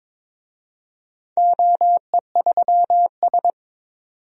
Morse code, heard as a keyed tone, about 22 wpm, 700 Hz. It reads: OE3S
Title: DXCC-100 Call Signs - 4 Characters - 22wpm